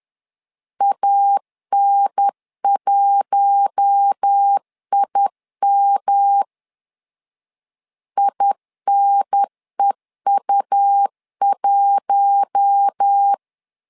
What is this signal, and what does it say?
Morse code, heard as a keyed tone, about 10 wpm, 800 Hz. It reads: AN1IM INEU1
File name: LEZIONI / 4 Q1 10wpm 2448